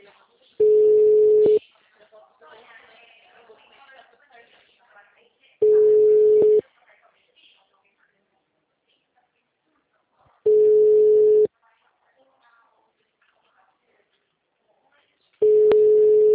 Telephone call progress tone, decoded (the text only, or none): ringback tone